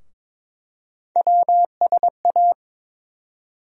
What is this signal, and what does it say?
Morse code, heard as a keyed tone, about 21 wpm, 700 Hz. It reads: WSA
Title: USA Call Sign Postfixes - 3 Letters 22wpm